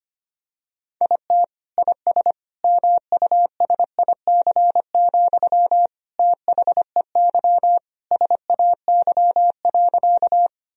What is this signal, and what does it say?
Morse code, heard as a keyed tone, about 25 wpm, 700 Hz. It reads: IT IS MUSIC, THEY SAY.
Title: Sentences from Top 1000 Words - Review Part 1 - 25wpm